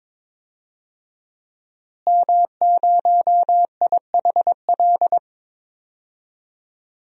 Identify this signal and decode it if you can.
Morse code, heard as a keyed tone, about 22 wpm, 700 Hz. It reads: M0IHL
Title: DXCC-100 Call Signs - 5 Characters - Repeated 3x - 22wpm